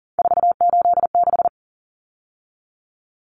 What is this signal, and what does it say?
Morse code, heard as a keyed tone, about 40 wpm, 700 Hz. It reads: 486
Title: Common Cut Numbers - 3-Digit 40wpm